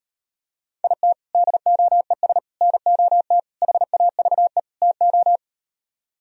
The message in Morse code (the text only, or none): IT DOES NOT HAVE TO